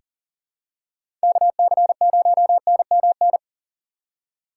Morse code, heard as a keyed tone, about 40 wpm, 700 Hz. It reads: KC0NMN
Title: USA Call Signs - 6 Characters - 4x Character Spacing - 40wpm